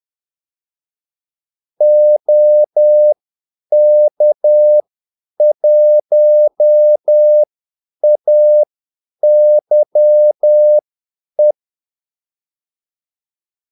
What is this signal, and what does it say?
Morse code, heard as a keyed tone, about 10 wpm, spaced as for 8 wpm, 600 Hz. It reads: OK1AYE
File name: Callsigns-10-8wpm